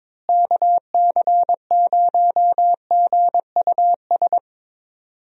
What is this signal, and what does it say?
Morse code, heard as a keyed tone, about 22 wpm, 700 Hz. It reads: KC0GUS
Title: USA Call Signs - 6 Characters - Repeated 2x - 22wpm